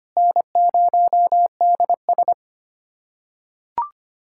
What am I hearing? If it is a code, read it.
Morse code, heard as a keyed tone, about 25 wpm, 700 Hz. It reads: N0DS
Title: USA Callsign DE Callsign - 4 Characters - 25wpm